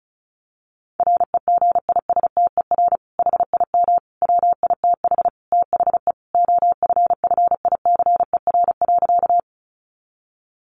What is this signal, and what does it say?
Morse code, heard as a keyed tone, about 34 wpm, 700 Hz. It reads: REGISTER HIM WITH THE OFFICER.
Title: Sentences from Top 1200 Words 35wpm